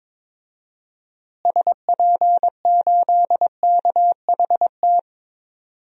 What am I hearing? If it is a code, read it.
Morse code, heard as a keyed tone, about 21 wpm, 700 Hz. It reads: SP8KHT
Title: DXCC-100 Call Signs - 6 Characters - Repeated 2x - 22wpm